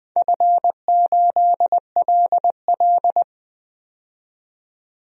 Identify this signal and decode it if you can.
Morse code, heard as a keyed tone, about 20 wpm, 700 Hz. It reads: F8LL
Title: DXCC-100 Call Signs - 4 Characters - 20wpm